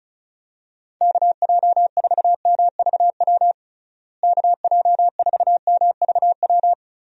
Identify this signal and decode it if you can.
Morse code, heard as a keyed tone, about 34 wpm, 700 Hz. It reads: KJ4MVW KJ4MVW
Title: USA Call Signs - 6 Characters - Repeated 3x - 35wpm